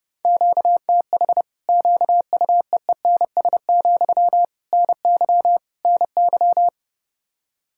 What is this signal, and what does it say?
Morse code, heard as a keyed tone, about 32 wpm, 700 Hz. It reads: QTH QUEENS, NY NY
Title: Practice QSO - Exchange 1 - Beginning - 30wpm